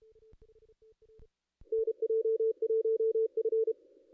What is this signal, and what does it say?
Morse code, heard as a keyed tone, about 30 wpm, 450 Hz. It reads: NJ1F
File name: contests / ss-cw-k5tr-op-n5rz -2200